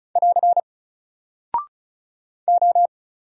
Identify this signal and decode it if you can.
Morse code, heard as a keyed tone, about 36 wpm, 700 Hz. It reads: <AR> O
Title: Single Character (Letter:Number:Punctuation:Prosign) Alt 35wpm